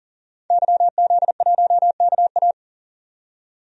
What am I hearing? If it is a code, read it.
Morse code, heard as a keyed tone, about 40 wpm, 700 Hz. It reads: YZ1KA
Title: DXCC-100 Call Signs - 5 Characters - 40wpm